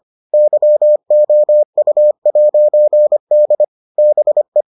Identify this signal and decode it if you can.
Morse code, heard as a keyed tone, about 25 wpm, 600 Hz. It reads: YOU'D BE